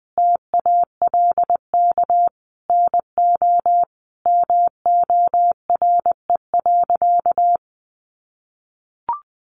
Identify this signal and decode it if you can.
Morse code, heard as a keyed tone, about 20 wpm, 700 Hz. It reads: TALK NO MORE.
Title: Sentences from Top 1400 Words 20wpm